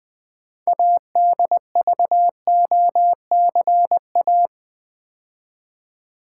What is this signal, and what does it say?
Morse code, heard as a keyed tone, about 20 wpm, 700 Hz. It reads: ADVOCA
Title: Build A Word - 2 Letters At A Time - 8-Letter Words - 20wpm